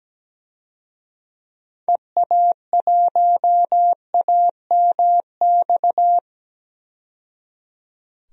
Morse code, heard as a keyed tone, about 17 wpm, 700 Hz. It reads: EA1AMX
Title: DXCC-100 Call Signs - 6 Characters - 17wpm